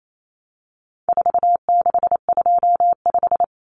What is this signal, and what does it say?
Morse code, heard as a keyed tone, about 28 wpm, 700 Hz. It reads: 4625
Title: Common Cut Numbers - 4-Digit 28wpm